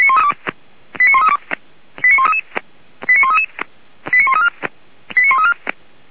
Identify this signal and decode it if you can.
Five-tone selective call (ZVEI), heard as quick runs of five tones, tones of about 80 ms. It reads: 89132 89133 89133 89134 89134